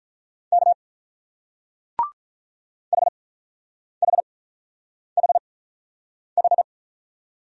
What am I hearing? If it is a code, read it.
Morse code, heard as a keyed tone, about 44 wpm, spaced as for 9 wpm, 690 Hz. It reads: K HHHH
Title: Speed-Racing Single Letters 35wpm